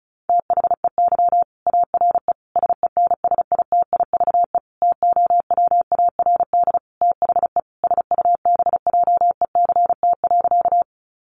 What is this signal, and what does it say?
Morse code, heard as a keyed tone, about 34 wpm, 700 Hz. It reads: THEY ARE SENSITIVE TOWARD THE SUBJECT.
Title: Sentences from Top 1400 Words 35wpm